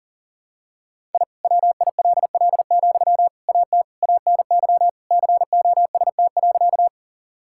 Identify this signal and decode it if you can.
Morse code, heard as a keyed tone, about 40 wpm, 700 Hz. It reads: I WILL, AT ANY COST.
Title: Sentences from Top 1000 Words - Review Part 5 - 40wpm